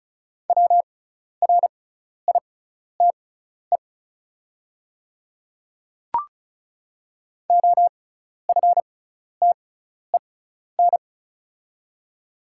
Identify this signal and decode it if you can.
Morse code, heard as a keyed tone, about 34 wpm, spaced as for 12 wpm, 700 Hz. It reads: WRITE OFTEN
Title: 6x Character Spacing - Top 300 Words - 35wpm